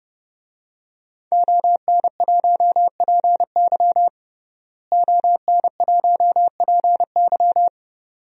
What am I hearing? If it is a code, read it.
Morse code, heard as a keyed tone, about 30 wpm, 700 Hz. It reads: ON1PY ON1PY
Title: DXCC-100 Call Signs - 5 Characters - Repeated 2x - 30wpm